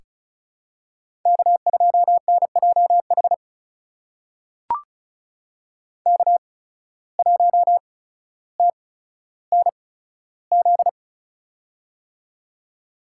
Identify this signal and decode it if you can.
Morse code, heard as a keyed tone, about 36 wpm, spaced as for 15 wpm, 700 Hz. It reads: K2NJH K1TNZ
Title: USA Call Signs - 5 Characters - 8x Character Spacing - 35wpm